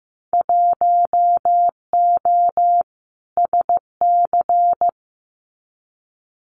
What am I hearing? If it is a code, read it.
Morse code, heard as a keyed tone, about 15 wpm, 700 Hz. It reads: 1O SC
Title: Winter Field Day Exchanges 15wpm